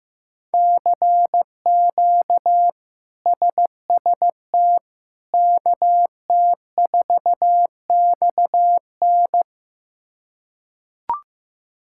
Morse code, heard as a keyed tone, about 15 wpm, 700 Hz. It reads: CQ SST KT4XN
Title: K1USN SST QSO Practice - 15wpm